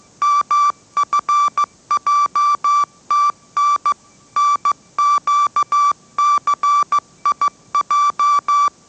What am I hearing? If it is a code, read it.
Morse code, heard as a keyed tone, about 15 wpm, 1160 Hz. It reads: MFJTN NQCIJ
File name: mrx / mrx065 16wpm